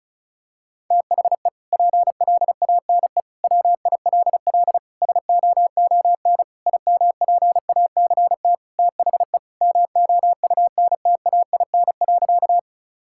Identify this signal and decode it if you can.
Morse code, heard as a keyed tone, about 36 wpm, 700 Hz. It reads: THE PLANE WILL SOON IMPACT THE MOUNTAIN.
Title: Sentences from Top 1000 Words - Review Part 6 - 35wpm